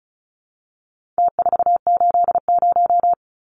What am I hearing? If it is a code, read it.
Morse code, heard as a keyed tone, about 34 wpm, 700 Hz. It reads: T480
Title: Numbers - Contest Style - 4-Digit 35wpm